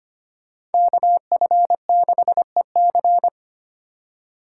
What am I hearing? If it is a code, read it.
Morse code, heard as a keyed tone, about 30 wpm, 700 Hz. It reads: KF6EC
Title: Speed Racing - USA Call Signs 25wpm